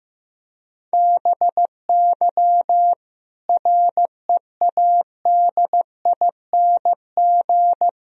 Morse code, heard as a keyed tone, about 15 wpm, 700 Hz. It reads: BY READING
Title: Sentences from Top 1000 Words - Review Part 3 - 15wpm